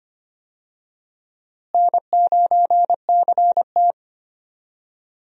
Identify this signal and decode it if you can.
Morse code, heard as a keyed tone, about 25 wpm, 700 Hz. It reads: N9CT
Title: USA Call Signs - 4 Characters - Repeated 3x - 25wpm